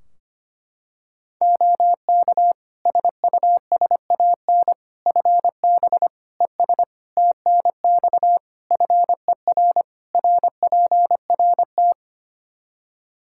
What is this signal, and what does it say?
Morse code, heard as a keyed tone, about 25 wpm, 700 Hz. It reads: OK SUSAN FB ES TNX FER RPRT
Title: Practice QSO - Exchange 2 - Info - 25wpm